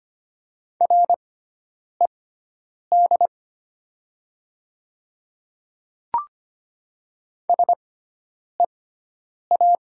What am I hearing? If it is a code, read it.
Morse code, heard as a keyed tone, about 25 wpm, spaced as for 9 wpm, 690 Hz. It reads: RED SEA
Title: 6x Character Spacing - Top 500 Words - 25wpm